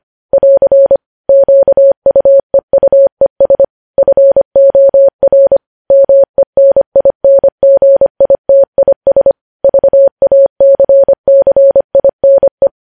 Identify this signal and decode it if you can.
Morse code, heard as a keyed tone, about 25 wpm, 550 Hz. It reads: <AR> QUEUES FOR MENINGITIS VACCINE
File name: podcast / News headlines-1x-25x25